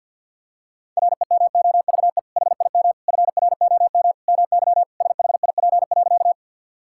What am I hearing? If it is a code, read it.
Morse code, heard as a keyed tone, about 45 wpm, 700 Hz. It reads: REMOVE HIM FROM MY SHIP.
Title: Sentences from Top 1000 Words - Review Part 5 - 50wpm